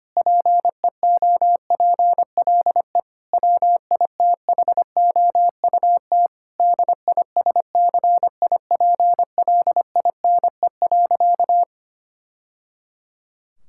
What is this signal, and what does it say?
Morse code, heard as a keyed tone, about 25 wpm, 700 Hz. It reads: PEOPLE WITHOUT DISCIPLINE.
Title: Sentences from Top 1000 Words - Review Part 6 - 25wpm